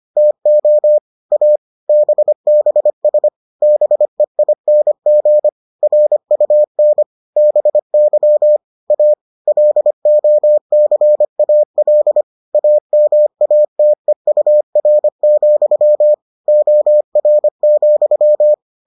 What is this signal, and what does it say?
Morse code, heard as a keyed tone, about 25 wpm, 600 Hz. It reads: TO A BBS BEING RUN BY A LOCAL AMATEUR, OR,